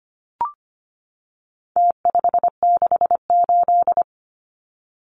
Morse code, heard as a keyed tone, about 25 wpm, 700 Hz. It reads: T568
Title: Numbers - Contest Style - 4-Digit 25wpm